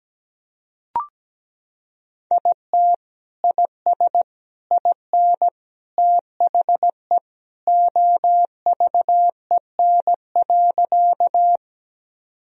Morse code, heard as a keyed tone, about 17 wpm, 700 Hz. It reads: IT IS IN THE OVEN.
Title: Sentences from Top 1000 Words - Review Part 4 - 17wpm